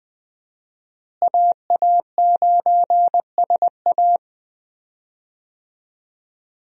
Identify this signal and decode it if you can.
Morse code, heard as a keyed tone, about 20 wpm, 700 Hz. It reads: AA9SA
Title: USA Call Signs - 5 Characters - Repeated 2x - 20wpm